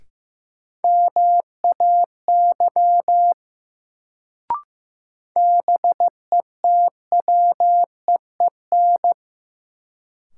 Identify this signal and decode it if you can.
Morse code, heard as a keyed tone, about 15 wpm, 700 Hz. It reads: MAY BETWEEN
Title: Sets of 2 Words - New Word Drill - 15wpm